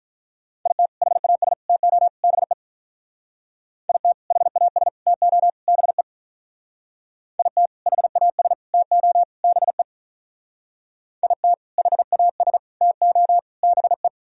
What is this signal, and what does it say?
Morse code, heard as a keyed tone, about 43 wpm, 700 Hz. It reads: IT HAS TO BE IT HAS TO BE IT HAS TO BE IT HAS TO BE
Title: Speed-Racing - Sets of 4 Words 35wpm